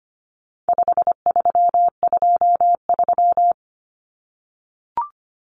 Morse code, heard as a keyed tone, about 25 wpm, 700 Hz. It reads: 5323
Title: Numbers - 4-Digit 25wpm